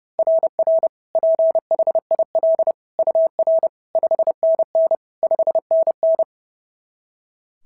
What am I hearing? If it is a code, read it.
Morse code, heard as a keyed tone, about 30 wpm, 690 Hz. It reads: RR PHIL UR 5NN 5NN
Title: SOTA QSO Practice 30wpm